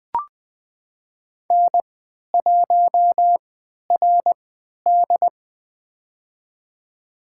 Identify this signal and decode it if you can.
Morse code, heard as a keyed tone, about 20 wpm, spaced as for 11 wpm, 700 Hz. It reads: N1RD
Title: USA Call Signs - 4 Characters - 3x Character Spacing - 20wpm